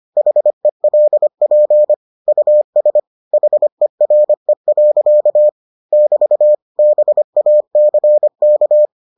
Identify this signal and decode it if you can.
Morse code, heard as a keyed tone, about 25 wpm, 600 Hz. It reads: HELP US HERE. <BT> BACK